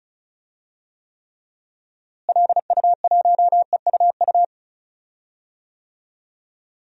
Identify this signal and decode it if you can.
Morse code, heard as a keyed tone, about 34 wpm, 700 Hz. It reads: LU1EUU
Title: DXCC-100 Call Signs - 6 Characters - Repeated 2x - 35wpm